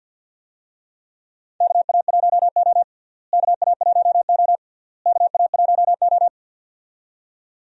Morse code, heard as a keyed tone, about 51 wpm, 700 Hz. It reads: KA1O KA1O KA1O
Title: USA Call Signs - 4 Characters - Repeated 3x - 50wpm